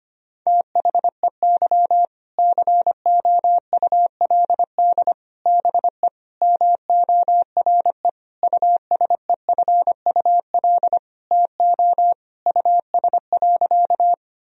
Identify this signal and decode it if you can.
Morse code, heard as a keyed tone, about 25 wpm, 700 Hz. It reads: THEY COULD BE MORE USEFUL TO US.
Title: Sentences from Top 1000 Words - Review Part 4 - 25wpm